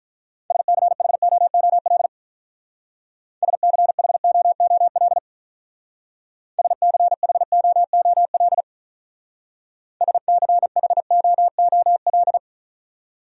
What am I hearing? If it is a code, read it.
Morse code, heard as a keyed tone, about 43 wpm, 700 Hz. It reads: SCHOOL SCHOOL SCHOOL SCHOOL
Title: Speed-Racing Top 300 Words 35wpm